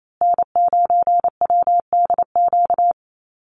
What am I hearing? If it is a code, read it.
Morse code, heard as a keyed tone, about 28 wpm, 700 Hz. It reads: N9WDQ
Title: USA Call Signs Repeated 2x 28wpm